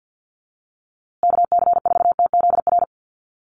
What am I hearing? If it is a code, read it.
Morse code, heard as a keyed tone, about 50 wpm, 700 Hz. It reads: KC4TZR
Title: USA Call Signs Repeated 3x 50wpm